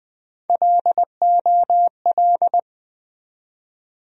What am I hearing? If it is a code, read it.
Morse code, heard as a keyed tone, about 20 wpm, 700 Hz. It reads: LOL
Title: USA Call Sign Postfixes - 3 Letters 20wpm